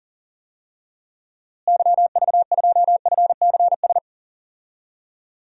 Morse code, heard as a keyed tone, about 40 wpm, 700 Hz. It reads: YV2FCS
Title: DXCC-100 Call Signs - 6 Characters - Repeated 2x - 40wpm